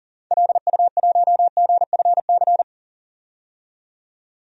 Morse code, heard as a keyed tone, about 40 wpm, 700 Hz. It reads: LU1GFC
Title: DXCC-100 Call Signs - 6 Characters - 40wpm